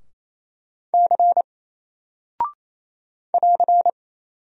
Morse code, heard as a keyed tone, about 28 wpm, 700 Hz. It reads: C <AR>
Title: Single Character (Letter:Number:Punctuation:Prosign) Alt 28wpm